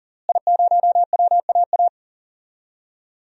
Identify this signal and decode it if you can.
Morse code, heard as a keyed tone, about 37 wpm, 700 Hz. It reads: I0WAA
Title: DXCC-100 Call Signs - 5 Characters - Repeated 2x - 40wpm